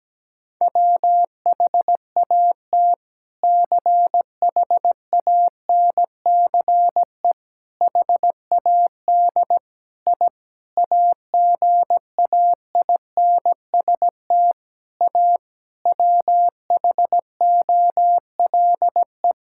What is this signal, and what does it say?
Morse code, heard as a keyed tone, about 17 wpm, 700 Hz. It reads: WHAT CHANCE HAD I AGAINST A WHOLE
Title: Sentences from Top 1000 Words - Review Part 6 - 17wpm